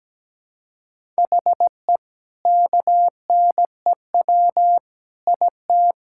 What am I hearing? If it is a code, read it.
Morse code, heard as a keyed tone, about 17 wpm, 700 Hz. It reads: HE KNEW IT
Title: Sentences from Top 1000 Words - Review Part 6 - 17wpm